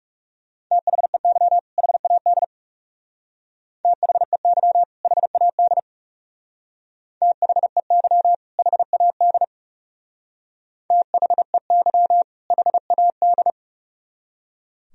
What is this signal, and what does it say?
Morse code, heard as a keyed tone, about 36 wpm, 700 Hz. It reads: THEY HAD THEY HAD THEY HAD THEY HAD
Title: Speed-Racing - Sets of 2 Words 30wpm